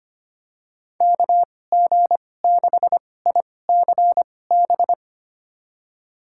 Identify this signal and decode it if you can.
Morse code, heard as a keyed tone, about 25 wpm, spaced as for 18 wpm, 700 Hz. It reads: KG6ICB
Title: USA Call Signs - 2x Character Spacing - 25wpm